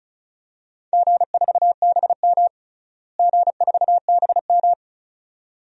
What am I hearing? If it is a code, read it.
Morse code, heard as a keyed tone, about 34 wpm, 700 Hz. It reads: G4BM G4BM
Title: DXCC-100 Call Signs - 4 Characters - Repeated 2x - 35wpm